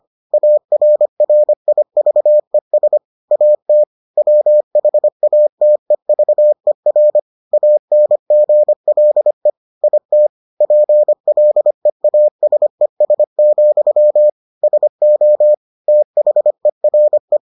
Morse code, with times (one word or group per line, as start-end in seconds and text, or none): 0.3-3.0 ARRIVES
3.3-3.8 AT
4.2-7.2 WHATEVER
7.5-9.5 ANGLE
9.8-10.3 IT
10.6-14.3 PLEASES,
14.6-15.5 SO
15.9-17.4 THERE